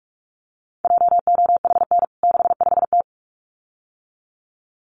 Morse code, heard as a keyed tone, about 44 wpm, 700 Hz. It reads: JOHN 65T
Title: CWT Exchange 45wpm